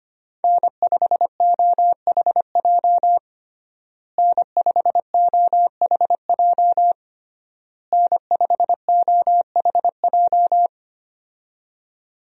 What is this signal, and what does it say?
Morse code, heard as a keyed tone, about 25 wpm, 700 Hz. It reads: N5OHJ N5OHJ N5OHJ
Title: USA Call Signs - 5 Characters - Repeated 3x - 25wpm